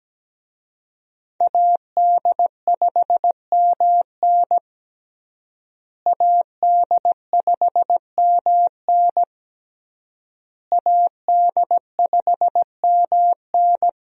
Morse code, heard as a keyed tone, about 17 wpm, 700 Hz. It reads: AD5MN AD5MN AD5MN
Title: USA Call Signs - 5 Characters - Repeated 3x - 17wpm